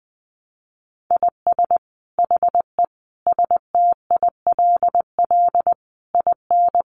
Morse code, heard as a keyed tone, about 20 wpm, 700 Hz. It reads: IS HE STILL IN
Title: Sentences from Top 800 Words 20wpm